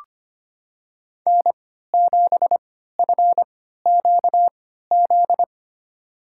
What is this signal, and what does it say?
Morse code, heard as a keyed tone, about 25 wpm, spaced as for 14 wpm, 700 Hz. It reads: N7FQZ
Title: USA Call Signs - 5 Characters - 3x Character Spacing - 25wpm